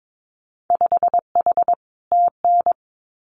Morse code, heard as a keyed tone, about 22 wpm, 700 Hz. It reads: 5H TN
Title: Winter Field Day Exchanges Encore 22wpm